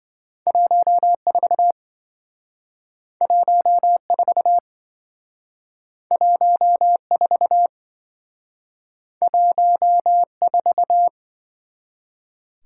Morse code, heard as a keyed tone, about 27 wpm, 700 Hz. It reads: 14 14 14 14